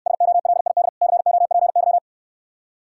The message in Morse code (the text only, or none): SOLID COPY